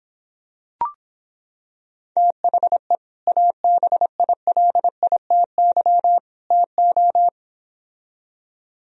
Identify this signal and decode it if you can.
Morse code, heard as a keyed tone, about 26 wpm, 700 Hz. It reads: THE ABILITY TO